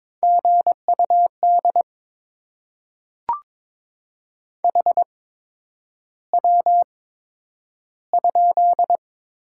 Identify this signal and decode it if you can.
Morse code, heard as a keyed tone, about 22 wpm, spaced as for 9 wpm, 700 Hz. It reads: GUD HW?
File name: Top 20 QSO Elements - 8x Character Spacing - 22wpm